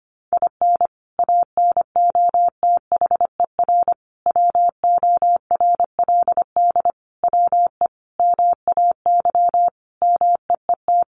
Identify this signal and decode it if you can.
Morse code, heard as a keyed tone, about 24 wpm, 700 Hz. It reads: IN ANOTHER WORLD WE MAY MEET
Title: Sentences from Top 1100 Words 25wpm